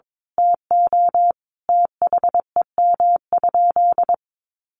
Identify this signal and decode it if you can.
Morse code, heard as a keyed tone, about 22 wpm, 700 Hz. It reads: TO THEM?
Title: Sentences from Top 1200 Words 22wpm